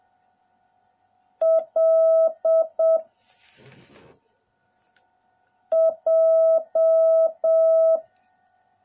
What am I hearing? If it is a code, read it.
Morse code, heard as a keyed tone, about 7 wpm, 650 Hz. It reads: L J